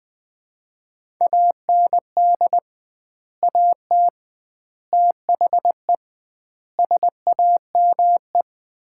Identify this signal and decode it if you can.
Morse code, heard as a keyed tone, about 20 wpm, 700 Hz. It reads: AND AT THE SAME
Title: 2x Word-Spacing - Sets of 5 Words - 20wpm